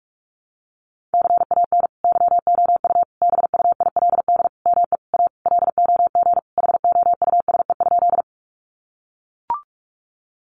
Morse code, heard as a keyed tone, about 44 wpm, 700 Hz. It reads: CAN YOU BUILD ME A LOG HOUSE?
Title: Sentences from Top 1400 Words 45wpm